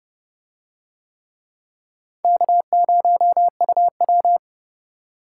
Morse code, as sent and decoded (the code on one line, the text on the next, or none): -.- ----- ..- .--
K0UW